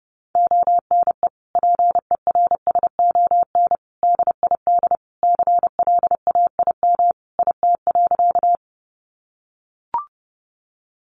Morse code, heard as a keyed tone, about 28 wpm, 700 Hz. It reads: ONE PERSON DID CLAIM IT.
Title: Sentences from Top 1500 Words 30wpm